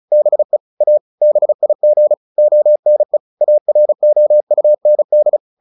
Morse code, heard as a keyed tone, about 34 wpm, 600 Hz. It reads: BE A BIG ONE AROUND
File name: the old man and the sea / the old man and the sea 1 35wpm